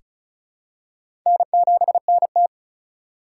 Morse code, heard as a keyed tone, about 34 wpm, 700 Hz. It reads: N7NT
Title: K1USN SST QSO Practice - 35wpm